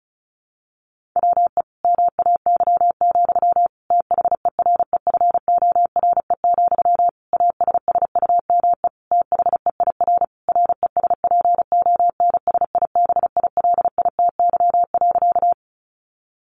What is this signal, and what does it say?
Morse code, heard as a keyed tone, about 34 wpm, 700 Hz. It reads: WE MAY, THEREFORE, ASSUME THEIR RESPONSIBILITY.
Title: Sentences from Top 1500 Words 35wpm